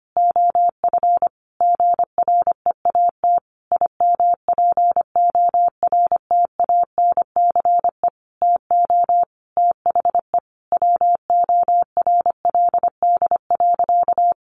Morse code, heard as a keyed tone, about 25 wpm, 700 Hz. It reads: OF GREAT IMPORTANCE TO THE WORLD.
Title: Sentences from Top 1200 Words 25wpm